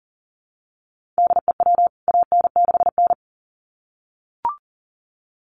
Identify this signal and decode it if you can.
Morse code, heard as a keyed tone, about 40 wpm, 700 Hz. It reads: DEW AN6N
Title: CWT Exchange 40wpm